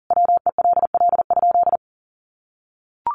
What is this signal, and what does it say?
Morse code, heard as a keyed tone, about 40 wpm, 700 Hz. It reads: WELL?
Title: Sentences from Top 400 Words 40wpm